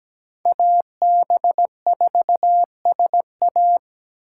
Morse code, heard as a keyed tone, about 20 wpm, 700 Hz. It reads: AB4SA
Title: Speed Racing - USA Call Signs 17wpm